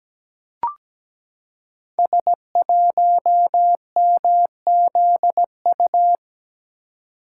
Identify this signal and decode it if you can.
Morse code, heard as a keyed tone, about 17 wpm, 700 Hz. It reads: S1MZU
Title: Code Groups - 5 Letter:Numbers - 17wpm